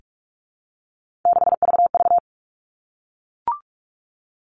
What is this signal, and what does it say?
Morse code, heard as a keyed tone, about 44 wpm, 700 Hz. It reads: 6VV
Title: All Cut Numbers - 3-Digit 45wpm